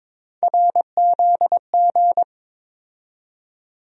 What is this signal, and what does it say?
Morse code, heard as a keyed tone, about 22 wpm, 700 Hz. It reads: RZG
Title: USA Call Sign Postfixes - 3 Letters 22wpm